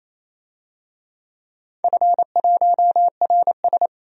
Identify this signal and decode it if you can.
Morse code, heard as a keyed tone, about 29 wpm, 690 Hz. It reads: F1RS
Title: DXCC-100 Call Signs - 4 Characters - 28wpm